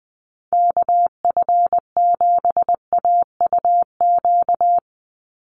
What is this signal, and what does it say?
Morse code, heard as a keyed tone, about 20 wpm, 700 Hz. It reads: KF7AUQ
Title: USA Call Signs 20wpm